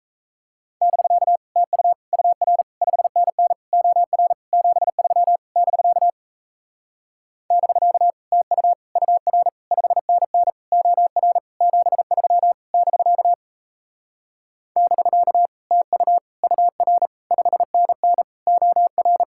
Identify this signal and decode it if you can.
Morse code, heard as a keyed tone, about 37 wpm, 700 Hz. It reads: <BK> TU UR 5NN OR 73 <BK> <BK> TU UR 5NN OR 73 <BK> <BK> TU UR 5NN OR